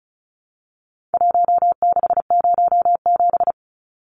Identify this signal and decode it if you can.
Morse code, heard as a keyed tone, about 34 wpm, 700 Hz. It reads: 1607
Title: Numbers - Contest Style - 4-Digit 35wpm